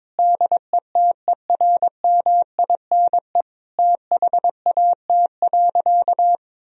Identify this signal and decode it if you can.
Morse code, heard as a keyed tone, about 22 wpm, 700 Hz. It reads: DETERMINE THAT.
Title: Sentences from Top 1000 Words - Review Part 4 - 22wpm